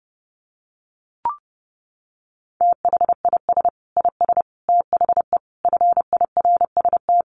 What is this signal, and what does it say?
Morse code, heard as a keyed tone, about 30 wpm, 700 Hz. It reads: THIS IS THE FIRST
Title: Sets of 5 Words 30wpm